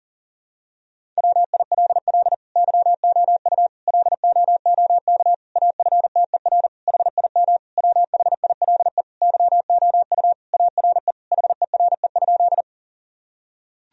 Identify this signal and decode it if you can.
Morse code, heard as a keyed tone, about 40 wpm, 700 Hz. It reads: WILL YOU LOOK AFTER HIM WHILE YOU ARE HERE?
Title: Sentences from Top 1000 Words - Review Part 1 - 40wpm